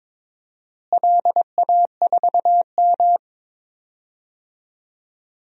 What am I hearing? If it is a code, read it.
Morse code, heard as a keyed tone, about 22 wpm, 700 Hz. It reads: LA4M
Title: DXCC-100 Call Signs - 4 Characters - Repeated 3x - 22wpm